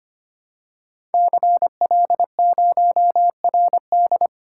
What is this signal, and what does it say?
Morse code, heard as a keyed tone, about 25 wpm, 700 Hz. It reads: CL0RD
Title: Code Groups - 5 Letter:Numbers - 25wpm